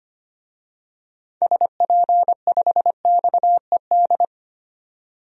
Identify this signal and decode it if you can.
Morse code, heard as a keyed tone, about 25 wpm, 700 Hz. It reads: SP5XED
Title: DXCC-100 Call Signs - 6 Characters - 25wpm